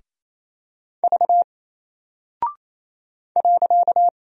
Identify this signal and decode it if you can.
Morse code, heard as a keyed tone, about 28 wpm, 700 Hz. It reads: V .
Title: Single Character (Letter:Number:Punctuation:Prosign) 28wpm